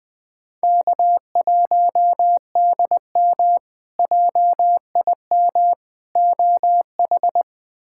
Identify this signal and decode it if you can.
Morse code, heard as a keyed tone, about 21 wpm, 700 Hz. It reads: K1DM JIM OH
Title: K1USN SST QSO Practice - 20wpm